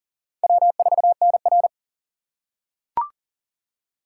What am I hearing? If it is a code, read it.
Morse code, heard as a keyed tone, about 40 wpm, 700 Hz. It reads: W4NR
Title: USA Call Signs - 4 Characters - 2x Character Spacing - 40wpm